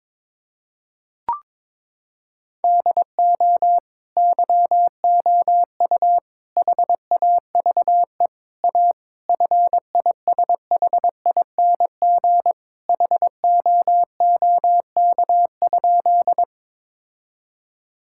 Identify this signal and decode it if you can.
Morse code, heard as a keyed tone, about 22 wpm, 700 Hz. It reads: DO YOU HAVE A FISHING HOOK?
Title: Sentences from Top 1600 Words 22wpm